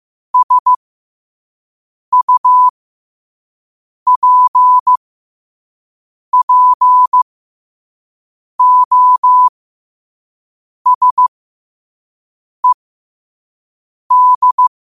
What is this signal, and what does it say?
Morse code, heard as a keyed tone, about 15 wpm, spaced as for 5 wpm, 1000 Hz. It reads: SUPPOSED
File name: qotdfiles / 5wpm